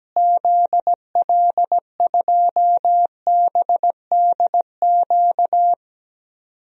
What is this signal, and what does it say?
Morse code, heard as a keyed tone, about 17 wpm, 700 Hz. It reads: ZL2BDQ
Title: DXCC-100 Call Signs - 6 Characters - Repeated 2x - 17wpm